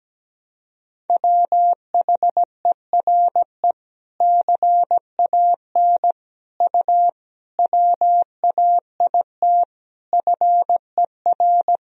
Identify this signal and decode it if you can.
Morse code, heard as a keyed tone, about 17 wpm, 700 Hz. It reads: WHERE CAN U WAIT FER
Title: Sentences from Top 500 Words and Abbreviations 17wpm